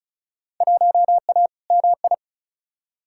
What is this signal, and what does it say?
Morse code, heard as a keyed tone, about 34 wpm, 700 Hz. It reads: 1A MI
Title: Field Day Exchanges - 35wpm